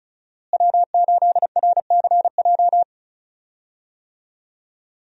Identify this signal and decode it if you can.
Morse code, heard as a keyed tone, about 34 wpm, 700 Hz. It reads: W8RCJ
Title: USA Call Signs - 5 Characters - Repeated 3x - 35wpm